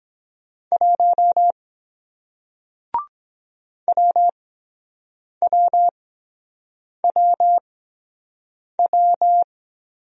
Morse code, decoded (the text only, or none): 1 WWWW